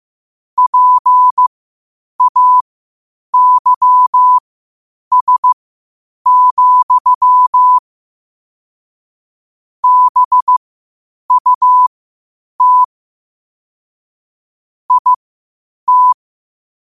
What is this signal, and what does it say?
Morse code, heard as a keyed tone, about 15 wpm, spaced as for 8 wpm, 1000 Hz. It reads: PAYS, BUT IT
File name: qotdfiles / 8wpm